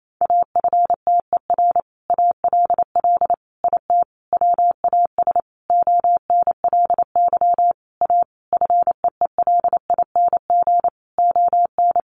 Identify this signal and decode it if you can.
Morse code, heard as a keyed tone, about 28 wpm, 700 Hz. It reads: AFTER ALL IT WAS ONLY A FEELING ON
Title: Sentences from Top 1200 Words 28wpm